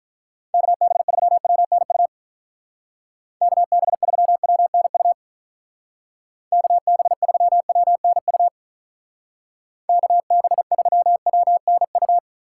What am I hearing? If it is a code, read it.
Morse code, heard as a keyed tone, about 45 wpm, 700 Hz. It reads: KB3WNU KB3WNU KB3WNU KB3WNU